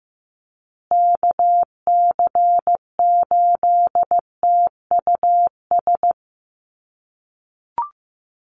Morse code, heard as a keyed tone, about 15 wpm, 700 Hz. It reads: KC8TUS
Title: USA Call Signs Repeated 2x 15wpm